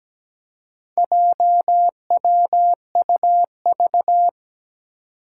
Morse code, heard as a keyed tone, about 17 wpm, 700 Hz. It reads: JWUV
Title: Four Challenging Letter-Number Combinations 17wpm